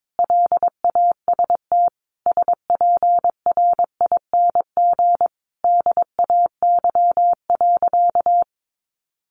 Morse code, heard as a keyed tone, about 22 wpm, 700 Hz. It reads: LAST SPRING DAY.
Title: Sentences from Top 1200 Words 22wpm